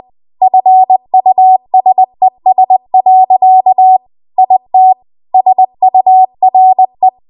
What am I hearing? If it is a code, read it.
Morse code, heard as a keyed tone, about 20 wpm, 750 Hz. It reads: FUSES. IT SURE
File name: Archive / 20 WPM / 20WPM